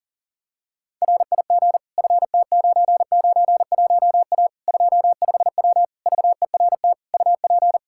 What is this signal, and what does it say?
Morse code, heard as a keyed tone, about 40 wpm, 700 Hz. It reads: RIG FT991A 25W VERT UP
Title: Practice QSO - Exchange 2 - Info - 40wpm